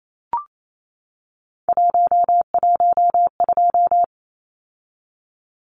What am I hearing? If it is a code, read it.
Morse code, heard as a keyed tone, about 28 wpm, 700 Hz. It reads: 112
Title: Numbers - 3-Digit 28wpm